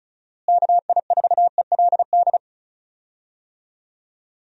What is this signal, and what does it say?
Morse code, heard as a keyed tone, about 34 wpm, 700 Hz. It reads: KI4ELD
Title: USA Call Signs - 6 Characters - Repeated 3x - 35wpm